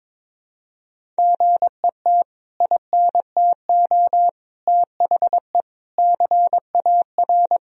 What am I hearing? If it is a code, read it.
Morse code, heard as a keyed tone, about 21 wpm, 700 Hz. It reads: GET INTO THE CAR
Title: Sentences from Top 1000 Words - Review Part 1 - 22wpm